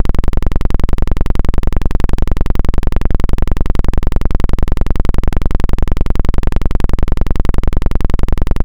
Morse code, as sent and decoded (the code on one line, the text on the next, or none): none